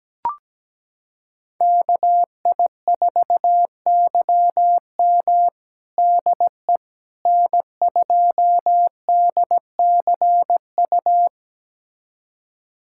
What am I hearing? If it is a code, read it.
Morse code, heard as a keyed tone, about 17 wpm, 700 Hz. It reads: KI4YM DE N2DCU
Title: USA Callsign DE Callsign - 5 Characters - 17wpm